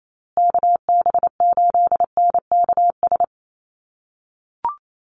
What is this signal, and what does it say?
Morse code, heard as a keyed tone, about 28 wpm, 700 Hz. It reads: KB8NKS
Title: USA Call Signs Repeated 3x 28wpm